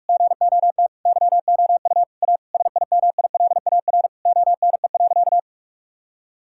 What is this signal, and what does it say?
Morse code, heard as a keyed tone, about 46 wpm, 700 Hz. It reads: GOT YOU A SIMILAR ONE.
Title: Sentences from Top 1000 Words - Review Part 5 - 45wpm